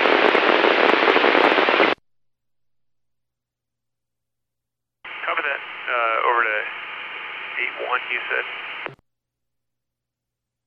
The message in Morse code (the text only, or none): none